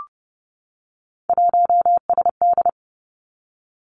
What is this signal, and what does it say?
Morse code, heard as a keyed tone, about 30 wpm, 700 Hz. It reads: 1SD
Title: Three Challenging Letter-Number Combinations 30wpm